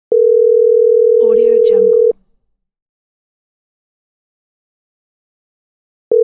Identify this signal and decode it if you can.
Busy tone (North America), 480 + 620 Hz, with speech.